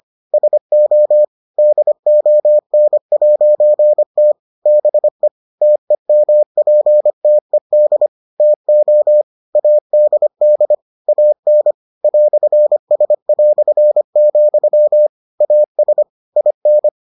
Morse code, heard as a keyed tone, about 25 wpm, 600 Hz. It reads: SO DON'T BE TEMPTED TO ADD AN "S", AS IN